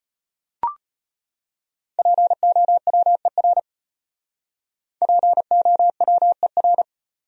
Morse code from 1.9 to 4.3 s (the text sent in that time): POWER